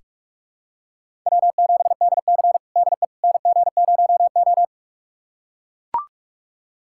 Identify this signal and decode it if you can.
Morse code, heard as a keyed tone, about 43 wpm, 700 Hz. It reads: W7DC DE NG0O
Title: USA Callsign DE Callsign - 4 Characters - 45wpm